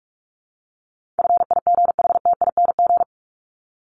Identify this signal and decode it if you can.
Morse code, heard as a keyed tone, about 44 wpm, 700 Hz. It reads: FIGHTING
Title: ING Suffix - Top 100 Words with Base Words 45wpm